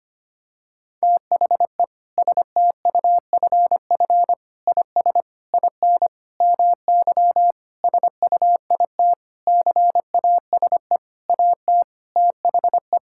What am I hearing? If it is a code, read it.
Morse code, heard as a keyed tone, about 25 wpm, 700 Hz. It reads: THE STUFF IS IN MY SUIT CASE AT THE
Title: Sentences from Top 1600 Words 25wpm